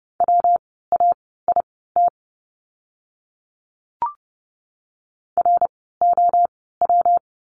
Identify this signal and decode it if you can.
Morse code, heard as a keyed tone, about 30 wpm, spaced as for 17 wpm, 700 Hz. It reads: WAIT ROW
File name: course / 30:10wpm / 115 Lesson 19f - Letter w - Words 30wpm